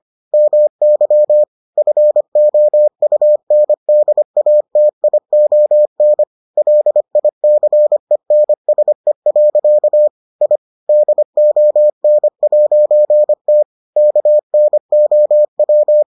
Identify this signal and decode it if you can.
Morse code, heard as a keyed tone, about 25 wpm, 600 Hz. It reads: MY FOUNDATION LICENSE. I DON'T KNOW